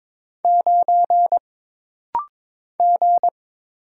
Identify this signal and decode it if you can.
Morse code, heard as a keyed tone, about 22 wpm, 700 Hz. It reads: 9 G
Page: This rapid-fire timing is geared to push your ICR (Instant Character Recognition) skills.